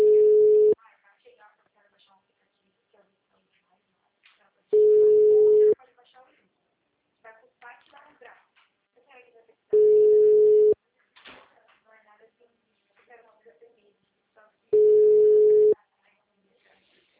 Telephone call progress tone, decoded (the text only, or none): ringback tone